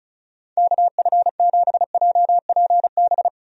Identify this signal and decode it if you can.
Morse code, heard as a keyed tone, about 36 wpm, 700 Hz. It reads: KF7JPB
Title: USA Call Signs - 6 Characters - 35wpm